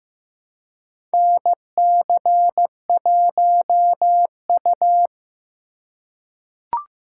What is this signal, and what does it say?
Morse code, heard as a keyed tone, about 15 wpm, 700 Hz. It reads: NC1U
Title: USA Call Signs - 4 Characters - 4x Character Spacing - 15wpm